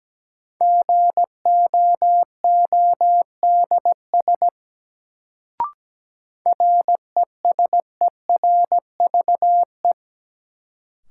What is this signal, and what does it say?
Morse code, heard as a keyed tone, about 17 wpm, 700 Hz. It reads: GOODS RESERVE
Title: Sentences from Top 1000 Words - Review Part 6 - New Word Drill - 17wpm